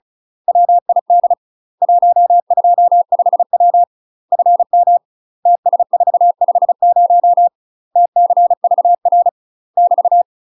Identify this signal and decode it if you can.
Morse code, heard as a keyed tone, about 37 wpm, 700 Hz. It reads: WID 125W FM TS450 TCVR <BT>
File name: UK MASTER QSOs TRAINING / 35WPM / UK MASTER QSOs 0015